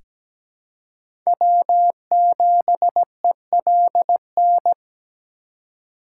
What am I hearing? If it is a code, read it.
Morse code, heard as a keyed tone, about 17 wpm, 700 Hz. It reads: W7ELN
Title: USA Call Signs - 3x Character Spacing - 17wpm